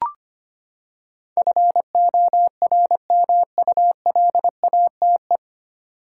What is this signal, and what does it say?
Morse code, heard as a keyed tone, about 25 wpm, 700 Hz. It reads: FORMULATE
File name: Nine Letter Words - 25wpm